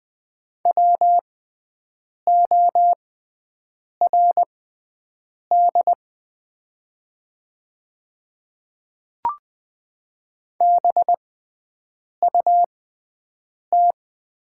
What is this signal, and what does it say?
Morse code, heard as a keyed tone, about 20 wpm, spaced as for 7 wpm, 700 Hz. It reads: WORD BUT
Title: 6x Character Spacing - Top 100 Words - 20wpm